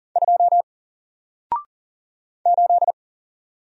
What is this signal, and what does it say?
Morse code, heard as a keyed tone, about 40 wpm, 700 Hz. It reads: 2 8
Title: Single Letter-Number Alt 40wpm